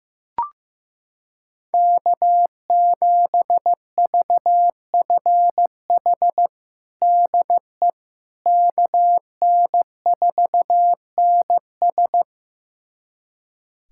Morse code, heard as a keyed tone, about 15 wpm, 700 Hz. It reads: K7VFH DE KN4NS
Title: USA Callsign DE Callsign - 5 Characters - 15wpm